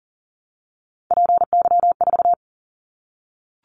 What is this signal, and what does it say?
Morse code, heard as a keyed tone, about 42 wpm, 700 Hz. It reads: PY4
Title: Callsign Prefixes for Top 50 DXCC - 40wpm